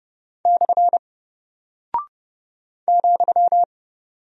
Morse code, heard as a keyed tone, about 30 wpm, 700 Hz. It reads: / ,
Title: Single Character (Letter:Number:Punctuation:Prosign) Alt 30wpm